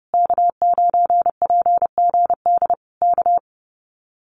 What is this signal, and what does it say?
Morse code, heard as a keyed tone, about 32 wpm, 700 Hz. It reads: K9PGD K
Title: Common QSO Phrases Encore 30wpm